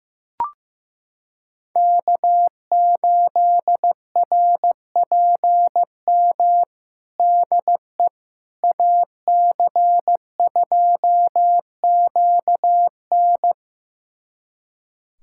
Morse code, heard as a keyed tone, about 15 wpm, 700 Hz. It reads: K8RPM DE AC2QN
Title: USA Callsign DE Callsign - 5 Characters - 15wpm